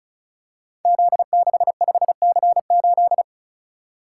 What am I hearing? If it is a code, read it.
Morse code, heard as a keyed tone, about 36 wpm, 690 Hz. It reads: Z65C8
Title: Code Groups - 5 Letter:Numbers - 35wpm